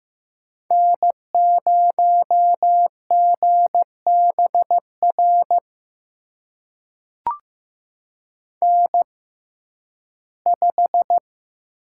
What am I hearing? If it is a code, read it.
Morse code, heard as a keyed tone, about 15 wpm, spaced as for 11 wpm, 700 Hz. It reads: N0GBR N5
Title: USA Call Signs - 5 Characters - 6x Character Spacing - 15wpm